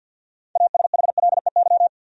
Morse code, heard as a keyed tone, about 49 wpm, 700 Hz. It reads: ASHLEY
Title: USA Common Names - 50wpm